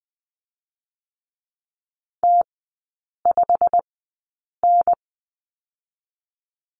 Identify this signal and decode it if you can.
Morse code, heard as a keyed tone, about 20 wpm, spaced as for 8 wpm, 700 Hz. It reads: T5N
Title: 152 Lesson 25e - Letter p - Three Characters 20wpm